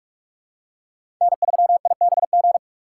Morse code, heard as a keyed tone, about 47 wpm, 700 Hz. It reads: N3IBG
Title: USA Call Signs - 5 Characters - 3x Character Spacing - 45wpm